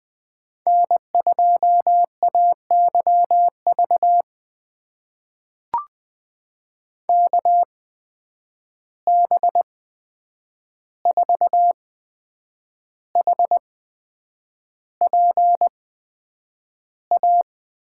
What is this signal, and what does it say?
Morse code, heard as a keyed tone, about 20 wpm, spaced as for 8 wpm, 700 Hz. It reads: N2AYV KB4HPA
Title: USA Call Signs - 8x Character Spacing - 20wpm